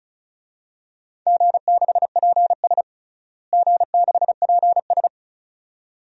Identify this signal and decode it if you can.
Morse code, heard as a keyed tone, about 36 wpm, 700 Hz. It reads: G6PS G6PS
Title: DXCC-100 Call Signs - 4 Characters - Repeated 2x - 35wpm